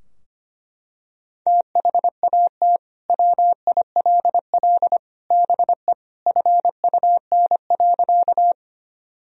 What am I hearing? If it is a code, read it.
Morse code, heard as a keyed tone, about 25 wpm, 700 Hz. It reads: THAT WILL BE FUN.
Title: Sentences from Top 1000 Words - Review Part 4 - 25wpm